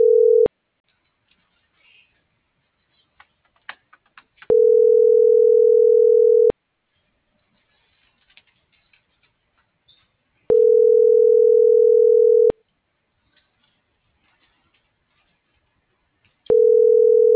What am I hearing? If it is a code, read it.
Ringback tone (North America), 440 + 480 Hz.
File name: CENTRAL SOTOVA / out alejandro.rodriguez - .52384